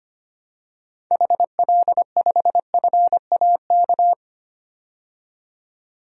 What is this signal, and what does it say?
Morse code, heard as a keyed tone, about 25 wpm, 700 Hz. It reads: HL5FAK
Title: DXCC-100 Call Signs - 6 Characters - 25wpm